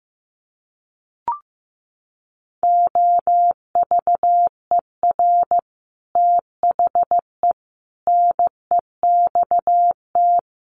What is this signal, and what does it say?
Morse code, heard as a keyed tone, about 15 wpm, 700 Hz. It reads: OVER THE NEXT
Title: Sets of 5 Words Encore 15wpm